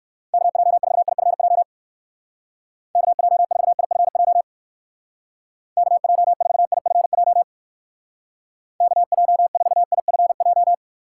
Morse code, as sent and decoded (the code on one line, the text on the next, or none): -.- .--- ....- .. ..-. .--- / -.- .--- ....- .. ..-. .--- / -.- .--- ....- .. ..-. .--- / -.- .--- ....- .. ..-. .---
KJ4IFJ KJ4IFJ KJ4IFJ KJ4IFJ